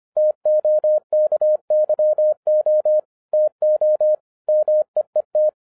Morse code, heard as a keyed tone, about 25 wpm, 610 Hz. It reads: TOKYO TO MEET